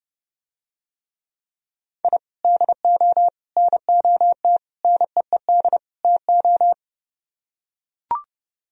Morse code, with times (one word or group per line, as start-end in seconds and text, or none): 2.0-2.2 I
2.4-3.3 DO
3.6-4.6 NOT
4.8-5.8 NEED
6.0-6.7 TO